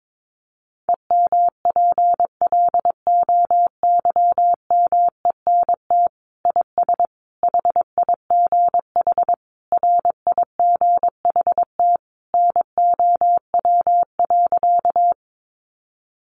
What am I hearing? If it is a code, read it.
Morse code, heard as a keyed tone, about 22 wpm, 700 Hz. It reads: EMPLOYMENT IS HIGH RIGHT NOW.
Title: Sentences from Top 1500 Words 22wpm